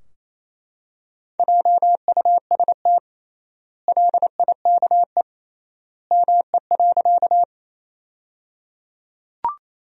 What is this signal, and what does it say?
Morse code, heard as a keyed tone, about 28 wpm, 700 Hz. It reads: JUST LIKE ME.
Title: 3x Word-Spacing - Sentences from Top 200 Words - 28wpm